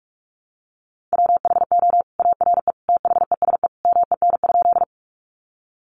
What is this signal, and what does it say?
Morse code, heard as a keyed tone, about 44 wpm, 700 Hz. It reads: WHO ARE THESE MEN?
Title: Sentences from Top 200 Words 45wpm